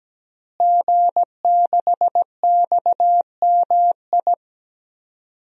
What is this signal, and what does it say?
Morse code, heard as a keyed tone, about 17 wpm, 700 Hz. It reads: G6XMI
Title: DXCC-100 Call Signs - 5 Characters - Repeated 2x - 17wpm